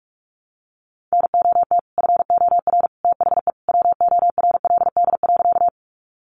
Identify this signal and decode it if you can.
Morse code, heard as a keyed tone, about 44 wpm, 700 Hz. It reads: NOT FOR THE WORLD.
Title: Sentences from Top 600 Words 45wpm